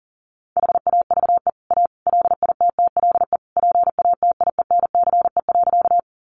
Morse code, heard as a keyed tone, about 40 wpm, 700 Hz. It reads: HAVE A LITTLE PATIENCE.
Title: Sentences from Top 1300 Words 40wpm